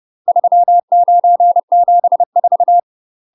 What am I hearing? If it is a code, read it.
Morse code, heard as a keyed tone, about 30 wpm, 700 Hz. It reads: 3974
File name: UK MASTER QSOs TRAINING / 30WPM / UK MASTER QSOs 0020